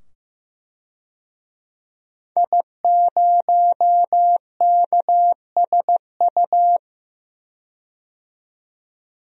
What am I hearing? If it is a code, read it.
Morse code, heard as a keyed tone, about 15 wpm, 700 Hz. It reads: I0KSU
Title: DXCC-100 Call Signs - 5 Characters - 15wpm